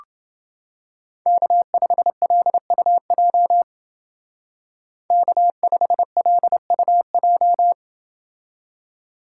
Morse code, heard as a keyed tone, about 28 wpm, 700 Hz. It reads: K5LUJ K5LUJ